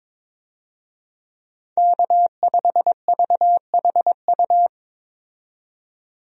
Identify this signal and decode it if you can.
Morse code, heard as a keyed tone, about 22 wpm, 700 Hz. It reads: K5VHU
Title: USA Call Signs - 5 Characters - 22wpm